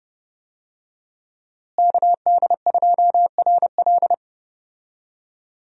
Morse code, heard as a keyed tone, about 31 wpm, 700 Hz. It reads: KD2RL
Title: USA Call Signs - 5 Characters - Repeated 2x - 30wpm